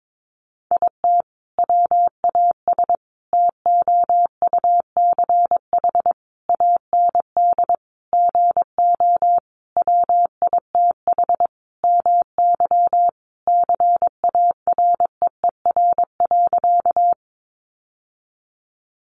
Morse code, heard as a keyed tone, about 22 wpm, 700 Hz. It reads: IT WAS TOUCH AND GO WITH MY CAREER.
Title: Sentences from Top 1200 Words 22wpm